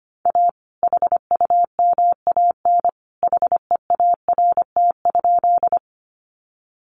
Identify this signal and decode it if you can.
Morse code, heard as a keyed tone, about 25 wpm, 700 Hz. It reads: A HUMAN HEART?
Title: Sentences from Top 1200 Words 25wpm